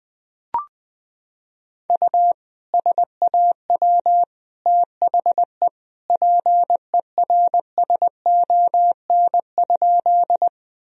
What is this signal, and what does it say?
Morse code, heard as a keyed tone, about 20 wpm, 700 Hz. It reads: U SAW THE PERSON?
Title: Sentences from Top 400 Words and Abbreviations 20wpm